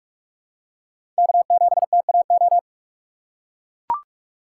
Morse code, heard as a keyed tone, about 44 wpm, 700 Hz. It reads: K7TAO
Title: USA Call Signs - 5 Characters - 6x Character Spacing - 45wpm